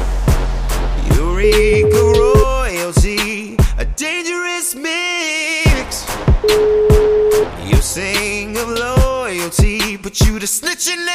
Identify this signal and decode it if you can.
Ringback tone (Europe, 425 Hz), with noise.